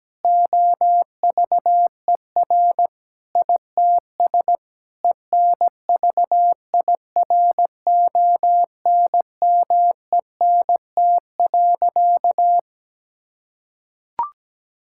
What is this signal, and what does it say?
Morse code, heard as a keyed tone, about 17 wpm, 700 Hz. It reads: OVER ITS ENVIRONMENT.
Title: Sentences from Top 1000 Words - Review Part 5 - 17wpm